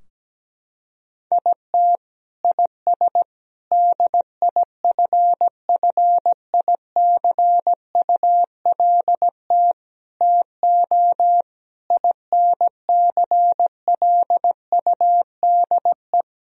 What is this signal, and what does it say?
Morse code, heard as a keyed tone, about 17 wpm, 700 Hz. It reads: IT IS DIFFICULT TO INCLUDE
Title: Sentences from Top 1000 Words - Review Part 3 - 17wpm